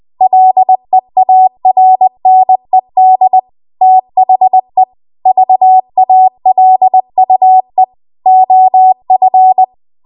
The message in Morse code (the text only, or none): LEARNED THE VALUE OF